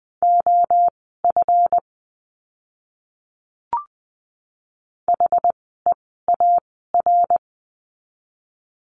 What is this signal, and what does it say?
Morse code, heard as a keyed tone, about 20 wpm, spaced as for 14 wpm, 700 Hz. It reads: OF HEAR
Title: 206 Lesson 33f - Letter j - Words 20wpm